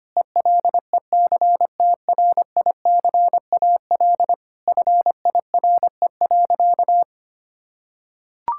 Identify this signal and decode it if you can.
Morse code, heard as a keyed tone, about 25 wpm, 700 Hz. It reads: ELECTRICAL FIRE.
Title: Sentences from Top 1000 Words - Review Part 6 - 25wpm